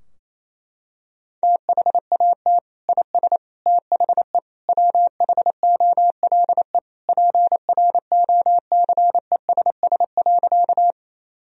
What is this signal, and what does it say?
Morse code, heard as a keyed tone, about 27 wpm, 700 Hz. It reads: THAT IS THE WHOLE PROCESS.
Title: Sentences from Top 1000 Words - Review Part 3 - 28wpm